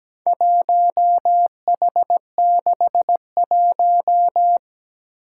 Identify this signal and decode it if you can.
Morse code, heard as a keyed tone, about 17 wpm, 700 Hz. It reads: 1H61
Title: Four Challenging Letter-Number Combinations 17wpm